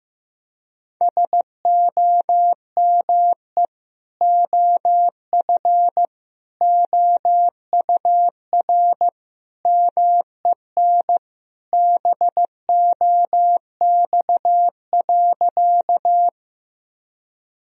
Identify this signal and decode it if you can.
Morse code, heard as a keyed tone, about 15 wpm, 700 Hz. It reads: SOME OF OUR MEN BOX.
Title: Sentences from Top 1000 Words - Review Part 2 - 15wpm